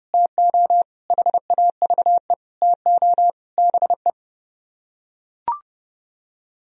Morse code, heard as a keyed tone, about 31 wpm, 700 Hz. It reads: TO HAVE TO BE